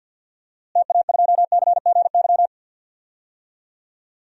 Morse code, heard as a keyed tone, about 53 wpm, 700 Hz. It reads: TA2CGY
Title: DXCC-100 Call Signs - 6 Characters - Repeated 2x - 50wpm